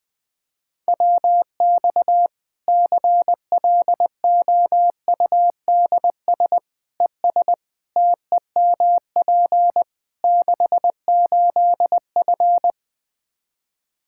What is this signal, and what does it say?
Morse code, heard as a keyed tone, about 21 wpm, 700 Hz. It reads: WX CLOUDS ES TEMP 68F
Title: Practice QSO - Exchange 2 - Info - 20wpm